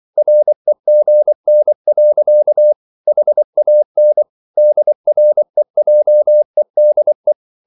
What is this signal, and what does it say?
Morse code, heard as a keyed tone, about 24 wpm, 600 Hz. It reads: REGN. HAN DREJEDE
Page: Ferie 24 wpm